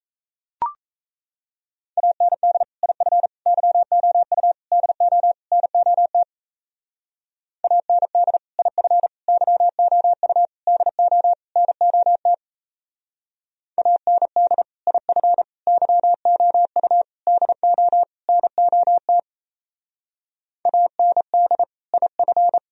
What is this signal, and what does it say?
Morse code, heard as a keyed tone, about 34 wpm, 700 Hz. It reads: AND IF YOU DO NOT AND IF YOU DO NOT AND IF YOU DO NOT AND IF
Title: Speed-Racing - Sets of 5 Words 28wpm